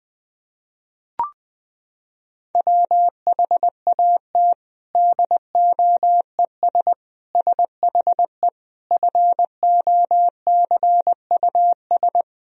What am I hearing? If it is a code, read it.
Morse code, heard as a keyed tone, about 20 wpm, 700 Hz. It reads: WHAT DOES SHE FOCUS
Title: Sentences from Top 1000 Words - Review Part 4 - 20wpm